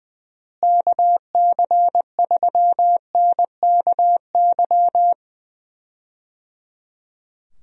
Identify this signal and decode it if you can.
Morse code, heard as a keyed tone, about 20 wpm, 700 Hz. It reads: KC3NKY
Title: USA Call Signs - 6 Characters - Repeated 3x - 20wpm